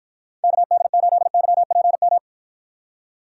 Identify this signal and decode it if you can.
Morse code, heard as a keyed tone, about 52 wpm, 700 Hz. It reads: KD8YPM